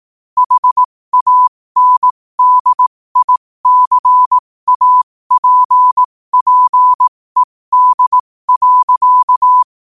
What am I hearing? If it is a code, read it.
Morse code, heard as a keyed tone, about 18 wpm, spaced as for 15 wpm, 1000 Hz. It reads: HANDICAPPED.
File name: qotdfiles / 15wpm